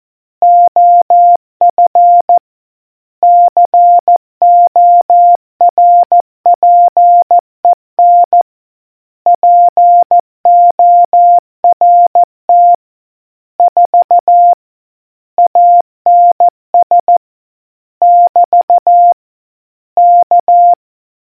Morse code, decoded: OF CORPEN PORT 4 ANS <BT> K